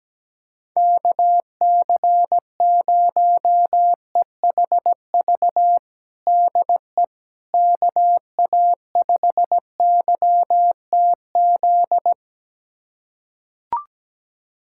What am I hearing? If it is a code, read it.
Morse code, heard as a keyed tone, about 17 wpm, 700 Hz. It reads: KC0EHV DE KA5YTZ
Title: USA Callsign DE Callsign - 6 Characters - 17wpm